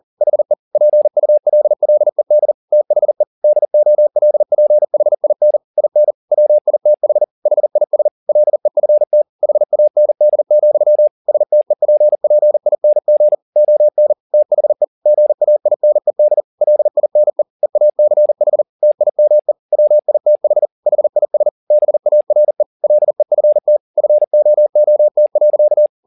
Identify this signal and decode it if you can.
Morse code, heard as a keyed tone, about 40 wpm, 600 Hz. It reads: HE PULLED THE DOLPHIN IN WITH HIS LEFT HAND, STEPPING ON THE GAINED LINE EACH TIME WITH HIS BARE LEFT FOOT.
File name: the old man and the sea 2 40wpm